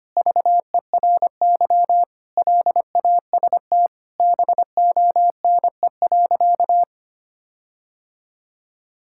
Morse code, heard as a keyed tone, about 25 wpm, 700 Hz. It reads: VERY LAST BONE.
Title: Sentences from Top 1600 Words 25wpm